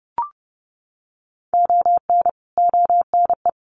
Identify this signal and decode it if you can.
Morse code, heard as a keyed tone, about 30 wpm, 700 Hz. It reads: ON ONE
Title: Sets of 5 Words 30wpm